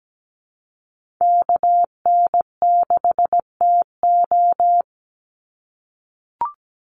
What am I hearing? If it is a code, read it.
Morse code, heard as a keyed tone, about 17 wpm, 700 Hz. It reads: KN6TO
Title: USA Call Signs 17wpm